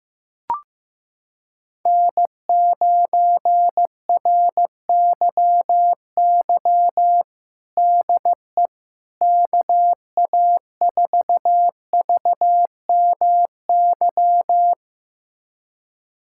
Morse code, heard as a keyed tone, about 15 wpm, 700 Hz. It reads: N9RYY DE KA4VMY
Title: USA Callsign DE Callsign 15wpm